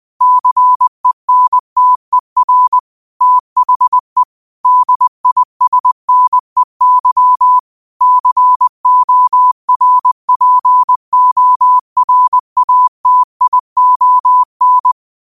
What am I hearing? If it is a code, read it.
Morse code, heard as a keyed tone, about 20 wpm, 1000 Hz. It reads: CENTER THE DISNEY CORPORATION
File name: qotdfiles / 20wpm